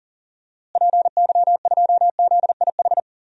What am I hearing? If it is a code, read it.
Morse code, heard as a keyed tone, about 40 wpm, 700 Hz. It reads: PY2ZIH
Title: DXCC-100 Call Signs - 6 Characters - Repeated 3x - 40wpm